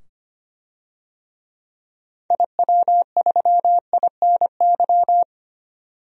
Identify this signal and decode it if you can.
Morse code, heard as a keyed tone, about 25 wpm, 700 Hz. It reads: IW3INY
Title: DXCC-100 Call Signs - 6 Characters - 25wpm